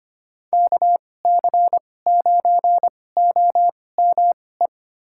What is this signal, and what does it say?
Morse code, heard as a keyed tone, about 25 wpm, spaced as for 18 wpm, 700 Hz. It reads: KC9OME
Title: USA Call Signs - 6 Characters - 2x Character Spacing - 25wpm